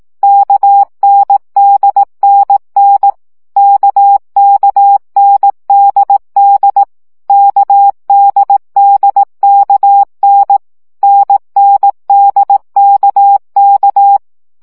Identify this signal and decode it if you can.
Morse code, heard as a keyed tone, about 18 wpm, 800 Hz. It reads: KNDNN KKNDD KDDKN NNDKK
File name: CW gyakorlas n d k 15 baud